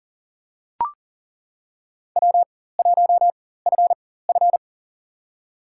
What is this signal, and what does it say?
Morse code, heard as a keyed tone, about 40 wpm, spaced as for 19 wpm, 700 Hz. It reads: W1FF
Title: USA Call Signs - 4x Character Spacing - 40wpm